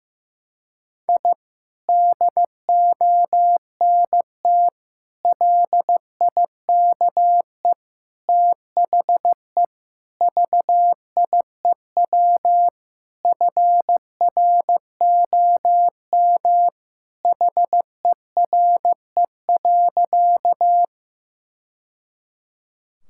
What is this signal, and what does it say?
Morse code, heard as a keyed tone, about 15 wpm, 700 Hz. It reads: I DONT LIKE THE VIEW FROM HERE.
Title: Sentences from Top 1000 Words - Review Part 6 - 15wpm